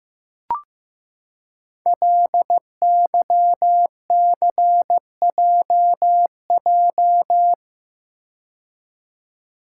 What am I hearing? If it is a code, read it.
Morse code, heard as a keyed tone, about 15 wpm, 700 Hz. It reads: LYCJJ
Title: Code Groups - 5 Letter:Numbers - 15wpm